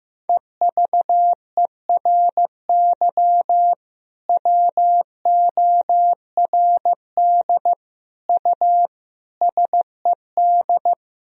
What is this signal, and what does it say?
Morse code, heard as a keyed tone, about 15 wpm, 700 Hz. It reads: EVERY WORD U SED
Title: Sentences from Top 400 Words and Abbreviations 15wpm